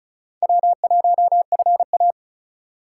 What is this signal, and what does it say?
Morse code, heard as a keyed tone, about 34 wpm, 700 Hz. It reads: W1FA
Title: USA Call Signs - 4 Characters - 4x Character Spacing - 35wpm